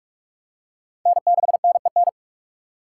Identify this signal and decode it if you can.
Morse code, heard as a keyed tone, about 40 wpm, 700 Hz. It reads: N6NEN